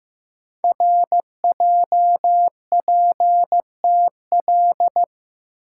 Morse code, heard as a keyed tone, about 15 wpm, 700 Hz. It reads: RJPTL
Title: Code Groups - 5 Letters - 15wpm